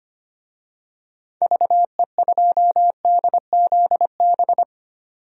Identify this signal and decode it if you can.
Morse code, heard as a keyed tone, about 25 wpm, 700 Hz. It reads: VE2DZB
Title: DXCC-100 Call Signs - 6 Characters - 25wpm